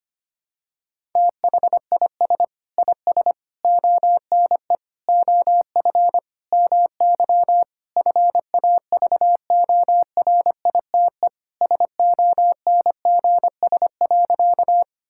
Morse code, heard as a keyed tone, about 25 wpm, 700 Hz. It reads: THIS IS ONE OF MY FAVORITE SONGS.
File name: practice / 25wpm / Sentences with 4-Word Phrases - 25wpm